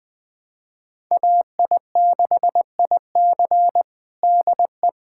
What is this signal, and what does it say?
Morse code, heard as a keyed tone, about 20 wpm, 700 Hz. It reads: AI6IC DE
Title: USA Callsign DE Callsign 20wpm